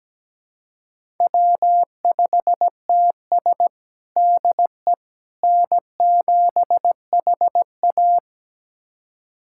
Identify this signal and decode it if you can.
Morse code, heard as a keyed tone, about 17 wpm, 700 Hz. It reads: W5TS DE N7HA
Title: USA Callsign DE Callsign - 4 Characters - 17wpm